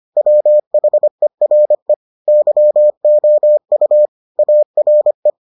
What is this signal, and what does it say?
Morse code, heard as a keyed tone, about 25 wpm, 600 Hz. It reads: WHERE YOU ARE